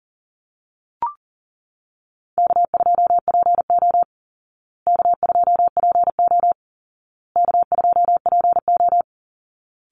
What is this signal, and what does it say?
Morse code, heard as a keyed tone, about 37 wpm, 700 Hz. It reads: K2PO K2PO K2PO
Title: Easy Call Signs Repeated 3x - 40wpm